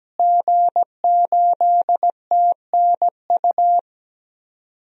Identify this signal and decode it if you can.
Morse code, heard as a keyed tone, about 17 wpm, 700 Hz. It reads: G8TNU
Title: DXCC-100 Call Signs - 5 Characters - Repeated 3x - 17wpm